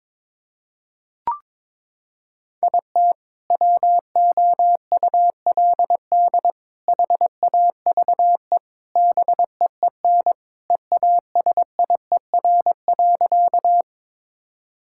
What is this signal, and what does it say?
Morse code, heard as a keyed tone, about 22 wpm, 700 Hz. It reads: IT WOULD HAVE BEEN EASIER.
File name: practice / 22wpm / Sentences with 4-Word Phrases - 22wpm